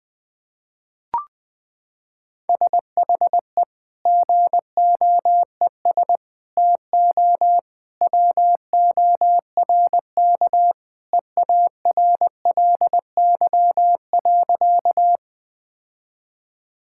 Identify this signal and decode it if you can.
Morse code, heard as a keyed tone, about 20 wpm, 700 Hz. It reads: SHE GOES TO WORK EARLY.
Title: Sentences with 2-Word Phrases - 20wpm